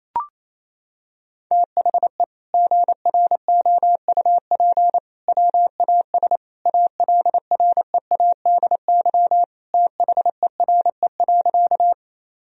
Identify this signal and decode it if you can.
Morse code, heard as a keyed tone, about 28 wpm, 700 Hz. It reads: THE GROUP WAS ALREADY THERE.
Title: Sentences from Top 1000 Words - Review Part 1 - 28wpm